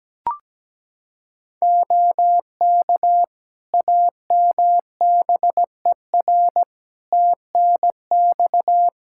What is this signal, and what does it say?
Morse code, heard as a keyed tone, about 17 wpm, 700 Hz. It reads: OK AMBER TNX
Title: Practice QSO - Exchange 4 - Ending - 17wpm